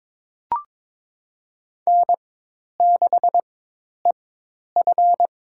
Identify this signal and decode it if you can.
Morse code, heard as a keyed tone, about 22 wpm, spaced as for 10 wpm, 700 Hz. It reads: N6EF
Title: USA Call Signs - 4x Character Spacing - 22wpm